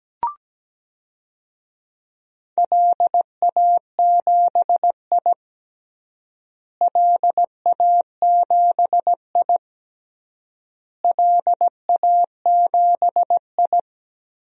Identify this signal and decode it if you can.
Morse code, heard as a keyed tone, about 17 wpm, 700 Hz. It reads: LA7I LA7I LA7I
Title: DXCC-100 Call Signs - 4 Characters - Repeated 3x - 17wpm